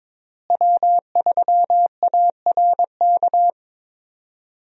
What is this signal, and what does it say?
Morse code, heard as a keyed tone, about 21 wpm, 700 Hz. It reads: W3ARK
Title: USA Call Signs - 5 Characters - Repeated 2x - 22wpm